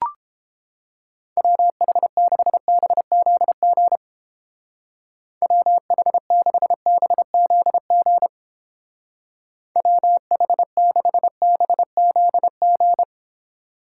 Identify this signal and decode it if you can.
Morse code, heard as a keyed tone, about 30 wpm, 700 Hz. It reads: WH6BZG WH6BZG WH6BZG